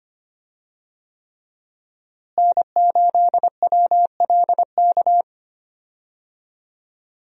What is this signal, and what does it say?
Morse code, heard as a keyed tone, about 25 wpm, 700 Hz. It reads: N8WLK
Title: USA Call Signs - 5 Characters - Repeated 3x - 25wpm